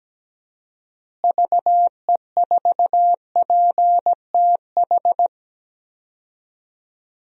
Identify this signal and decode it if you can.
Morse code, heard as a keyed tone, about 17 wpm, 700 Hz. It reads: VE4PTH
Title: DXCC-100 Call Signs - 6 Characters - Repeated 2x - 17wpm